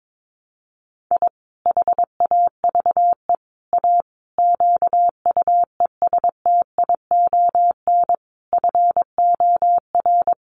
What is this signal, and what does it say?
Morse code, heard as a keyed tone, about 22 wpm, 700 Hz. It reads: I HAVE A QUESTION FOR
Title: Sets of 5 Words Encore2 22wpm